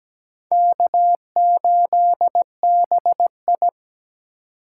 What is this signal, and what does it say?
Morse code, heard as a keyed tone, about 20 wpm, 700 Hz. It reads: K8BI
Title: Speed Racing - USA 1x2 Call Signs 17wpm